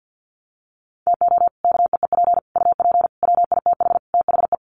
Morse code, heard as a keyed tone, about 50 wpm, 700 Hz. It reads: TO KEEP UP WITH THE
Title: Sets of 5 Words Encore 50wpm